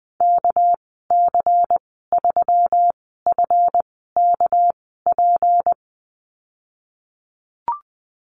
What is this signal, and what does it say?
Morse code, heard as a keyed tone, about 20 wpm, spaced as for 14 wpm, 700 Hz. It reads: KC3FKP
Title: 245 Lesson 39g - Letter z - Call Signs 20wpm